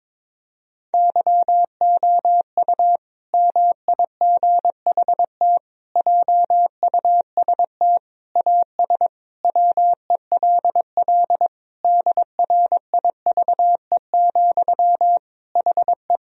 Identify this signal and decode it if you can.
Morse code, heard as a keyed tone, about 22 wpm, 700 Hz. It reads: YOU MIGHT JUST AS WELL DRIVE, HE
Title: Sentences from Top 1000 Words - Review Part 2 - 22wpm